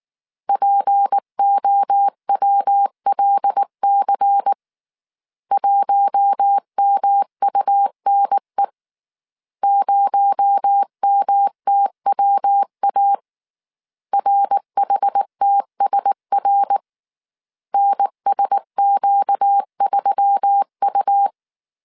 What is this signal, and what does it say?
Morse code, heard as a keyed tone, about 19 wpm, 800 Hz. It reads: POWLC 1MUNE 0MTWA RHTSR NSQ3U
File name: CW / DET / 1 6 1500C S C 18wpm 2356